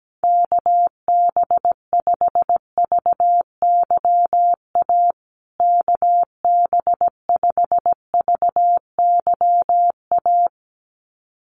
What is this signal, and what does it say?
Morse code, heard as a keyed tone, about 17 wpm, 700 Hz. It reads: KB5VYA KB5VYA
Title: USA Call Signs Repeated 2x 17wpm